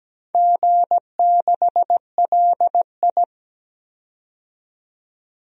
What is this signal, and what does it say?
Morse code, heard as a keyed tone, about 17 wpm, 700 Hz. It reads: G6LI
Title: DXCC-100 Call Signs - 4 Characters - 17wpm